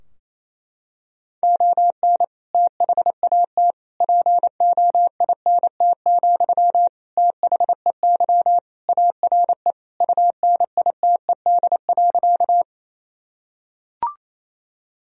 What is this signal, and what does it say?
Morse code, heard as a keyed tone, about 28 wpm, 700 Hz. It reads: ON THAT POINT, THEY ARE UNITED.
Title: Sentences from Top 1000 Words - Review Part 4 - 28wpm